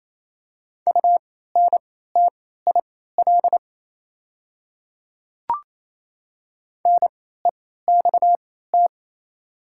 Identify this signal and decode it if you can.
Morse code, heard as a keyed tone, about 27 wpm, spaced as for 16 wpm, 700 Hz. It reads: UNTIL NEXT
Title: 3x Character Spacing - Top 300 Words - 28wpm